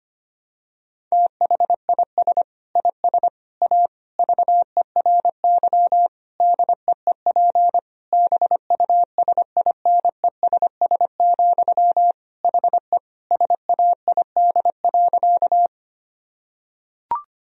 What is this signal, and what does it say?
Morse code, heard as a keyed tone, about 25 wpm, 700 Hz. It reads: THIS IS A VERY DEEP BUSINESS, HE SAID.
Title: Sentences from Top 1000 Words - Review Part 3 - 25wpm